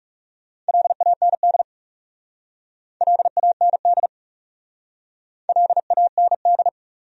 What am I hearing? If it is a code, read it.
Morse code, heard as a keyed tone, about 39 wpm, 700 Hz. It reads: LAND LAND LAND